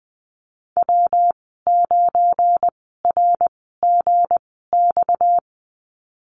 Morse code, heard as a keyed tone, about 20 wpm, spaced as for 14 wpm, 700 Hz. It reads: W9RGX
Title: 238 Lesson 38g - Letter q - Call Signs 20wpm